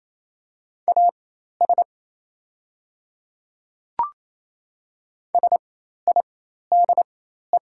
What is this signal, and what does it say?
Morse code, heard as a keyed tone, about 28 wpm, spaced as for 13 wpm, 700 Hz. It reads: AS SIDE
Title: 4x Character Spacing - Top 100 Words - 28wpm